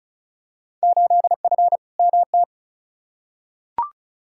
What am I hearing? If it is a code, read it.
Morse code, heard as a keyed tone, about 36 wpm, 700 Hz. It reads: 8F MT
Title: Field Day Exchanges Encore - 35wpm